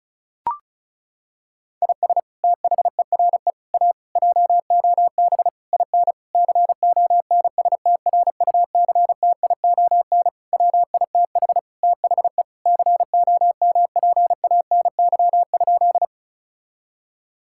Morse code, heard as a keyed tone, about 36 wpm, 700 Hz. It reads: IS THERE A JOB IN CONSTRUCTION WITH THE COMPANY?
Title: Sentences from Top 1600 Words 35wpm